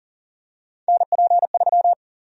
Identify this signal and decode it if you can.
Morse code, heard as a keyed tone, about 40 wpm, 700 Hz. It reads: NP3
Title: USA Call Sign Prefixes - Repeated 2x - 40wpm